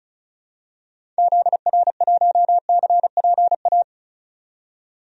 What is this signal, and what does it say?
Morse code, heard as a keyed tone, about 35 wpm, 700 Hz. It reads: ZR1CPA
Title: DXCC-100 Call Signs - 6 Characters - 35wpm